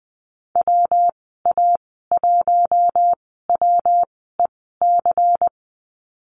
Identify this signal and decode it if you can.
Morse code, heard as a keyed tone, about 20 wpm, spaced as for 15 wpm, 700 Hz. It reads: WA1WEC
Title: 186 Lesson 30g - Letter b - Call Signs 20wpm